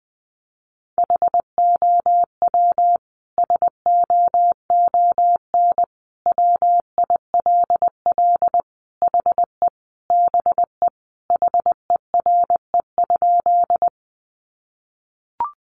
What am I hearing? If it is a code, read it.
Morse code, heard as a keyed tone, about 20 wpm, 700 Hz. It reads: HOW SOON WILL HE BE HERE?
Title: Sentences from Top 400 Words 20wpm